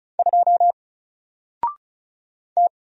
Morse code, heard as a keyed tone, about 35 wpm, 700 Hz. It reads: 2 T
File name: Single-Letter-Number-35wpm